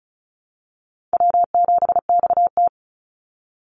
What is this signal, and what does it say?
Morse code, heard as a keyed tone, about 34 wpm, 700 Hz. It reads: W7XT
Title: Easy Call Signs Repeated 3x - 35wpm